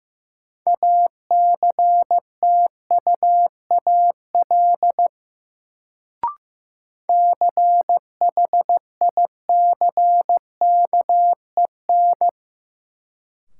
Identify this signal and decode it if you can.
Morse code, heard as a keyed tone, about 16 wpm, 700 Hz. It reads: ACTUAL CHICKEN
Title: Sentences from Top 1000 Words - Review Part 5 - New Word Drill - 15wpm